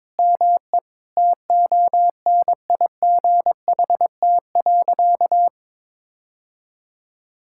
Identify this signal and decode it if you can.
Morse code, heard as a keyed tone, about 21 wpm, 700 Hz. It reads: ME TONIGHT.
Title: Sentences from Top 1000 Words - Review Part 5 - 22wpm